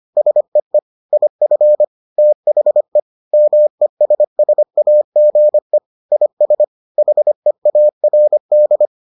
Morse code, heard as a keyed tone, about 25 wpm, 600 Hz. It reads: SEE IF THE MESSAGE IS HEARD